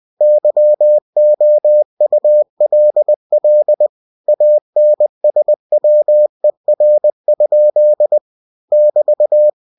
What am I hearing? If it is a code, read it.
Morse code, heard as a keyed tone, about 20 wpm, 600 Hz. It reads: YOULL ANSWER? <BT>
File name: winnie-20wpm